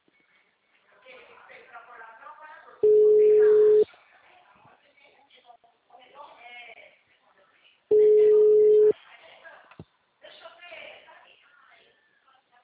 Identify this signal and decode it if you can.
Ringback tone (Europe, 425 Hz), with speech.